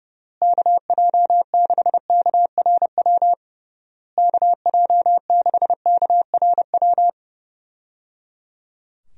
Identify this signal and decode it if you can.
Morse code, heard as a keyed tone, about 29 wpm, 700 Hz. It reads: KJ6KRW KJ6KRW
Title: USA Call Signs - 6 Characters - Repeated 3x - 30wpm